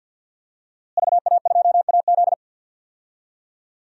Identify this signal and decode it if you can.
Morse code, heard as a keyed tone, about 52 wpm, 700 Hz. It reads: VA2AZ
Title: DXCC-100 Call Signs - 5 Characters - Repeated 3x - 50wpm